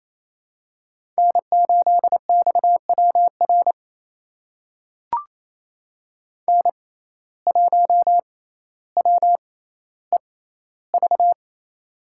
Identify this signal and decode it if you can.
Morse code, heard as a keyed tone, about 28 wpm, spaced as for 14 wpm, 700 Hz. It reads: N8XWR N1WEV
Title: USA Call Signs - 5 Characters - 6x Character Spacing - 28wpm